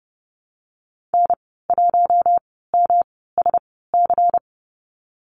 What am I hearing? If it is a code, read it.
Morse code, heard as a keyed tone, about 30 wpm, spaced as for 17 wpm, 700 Hz. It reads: N1MSC
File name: Lesson 19g - Letter w - Call Signs 30wpm